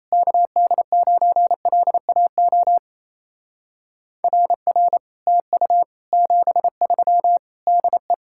Morse code, heard as a keyed tone, about 37 wpm, 700 Hz. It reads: KD9LAO RR TU 73 DE